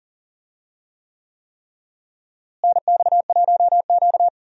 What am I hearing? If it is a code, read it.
Morse code, heard as a keyed tone, about 40 wpm, 700 Hz. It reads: NX1Q
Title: USA Call Signs - 4 Characters - Repeated 3x - 40wpm